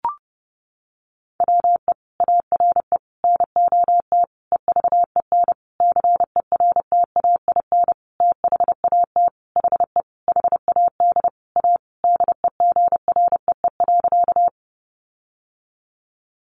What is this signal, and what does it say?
Morse code, heard as a keyed tone, about 30 wpm, 700 Hz. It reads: WE ARE NOT EVEN CERTAIN THAT HE HAD A DEGREE.
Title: Sentences from Top 1500 Words 30wpm